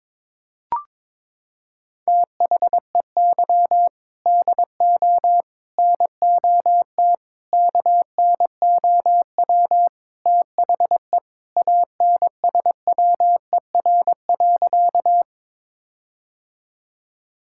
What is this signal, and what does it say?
Morse code, heard as a keyed tone, about 22 wpm, 700 Hz. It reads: THEY DO NOT KNOW THE ANSWER.
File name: practice / 22wpm / Sentences with 4-Word Phrases - 22wpm